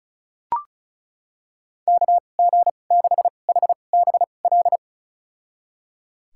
Morse code, heard as a keyed tone, about 36 wpm, spaced as for 25 wpm, 700 Hz. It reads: KG6HBL
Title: USA Call Signs - 2x Character Spacing - 35wpm